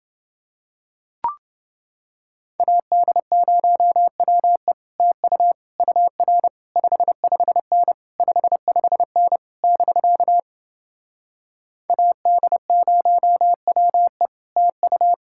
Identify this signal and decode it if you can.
Morse code, heard as a keyed tone, about 29 wpm, 700 Hz. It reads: AD0WE TU UR 55N 55N <BK> AD0WE TU